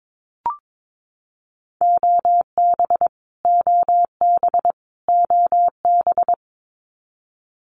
Morse code, heard as a keyed tone, about 22 wpm, 700 Hz. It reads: OB OB OB
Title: Abbreviations Repeated 3x - 22wpm